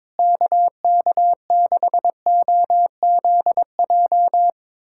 Morse code, heard as a keyed tone, about 21 wpm, 700 Hz. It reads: KK6OZJ
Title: USA Callsign DE Callsign 22wpm